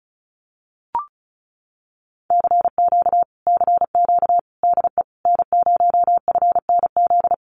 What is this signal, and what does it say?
Morse code, heard as a keyed tone, about 34 wpm, 700 Hz. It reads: CQ CQ DE N0FNZ
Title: Common QSO Phrases Encore 35wpm